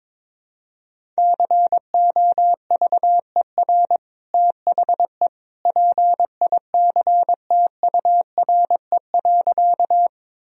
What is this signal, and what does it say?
Morse code, heard as a keyed tone, about 22 wpm, 700 Hz. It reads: COVER THE PICTURE.
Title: Sentences from Top 1000 Words - Review Part 1 - 22wpm